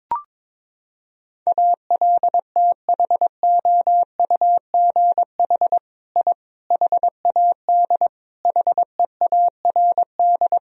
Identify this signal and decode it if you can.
Morse code, heard as a keyed tone, about 22 wpm, 700 Hz. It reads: ALTHOUGH I HAD HEARD
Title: Sentences from Top 1000 Words - Review Part 3 - 22wpm